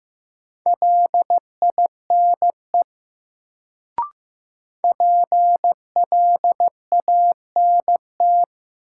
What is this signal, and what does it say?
Morse code, heard as a keyed tone, about 15 wpm, 700 Hz. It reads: LINE PLANT
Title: Sentences from Top 1000 Words - Review Part 1 - New Word Drill - 15wpm